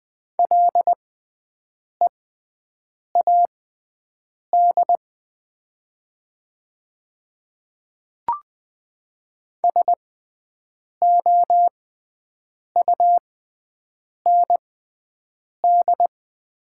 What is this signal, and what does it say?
Morse code, heard as a keyed tone, about 20 wpm, spaced as for 7 wpm, 700 Hz. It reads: LEAD SOUND
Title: 6x Character Spacing - Top 500 Words - 20wpm